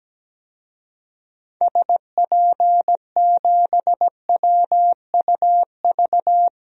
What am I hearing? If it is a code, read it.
Morse code, heard as a keyed tone, about 17 wpm, 700 Hz. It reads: SP7WUV
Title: DXCC-100 Call Signs - 6 Characters - 17wpm